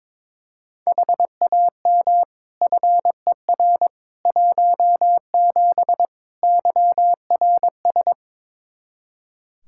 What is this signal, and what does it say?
Morse code, heard as a keyed tone, about 22 wpm, 700 Hz. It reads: HAM FER 17 YRS
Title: Practice QSO - Exchange 3 - More Info - 22wpm